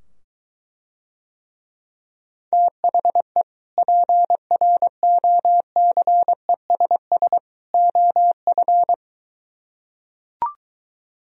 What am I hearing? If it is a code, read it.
Morse code, heard as a keyed tone, about 23 wpm, 700 Hz. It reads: THE PROCESS OF